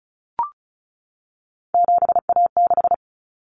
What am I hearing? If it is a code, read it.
Morse code, heard as a keyed tone, about 34 wpm, 700 Hz. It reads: 7A6
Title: Common Cut Numbers - 3-Digit 35wpm